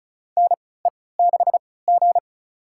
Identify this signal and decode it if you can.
Morse code, heard as a keyed tone, about 34 wpm, spaced as for 20 wpm, 700 Hz. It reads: NE6G
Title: USA Call Signs - 4 Characters - 3x Character Spacing - 35wpm